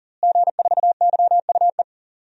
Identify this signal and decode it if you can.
Morse code, heard as a keyed tone, about 40 wpm, 700 Hz. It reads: G4YUE
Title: DXCC-100 Call Signs - 5 Characters - Repeated 2x - 40wpm